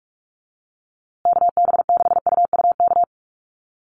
Morse code, heard as a keyed tone, about 44 wpm, 700 Hz. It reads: KB6UUK
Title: USA Call Signs Repeated 3x 45wpm